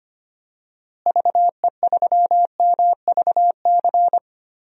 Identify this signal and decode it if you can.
Morse code, heard as a keyed tone, about 25 wpm, 700 Hz. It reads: VE3MVC
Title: DXCC-100 Call Signs - 6 Characters - Repeated 3x - 25wpm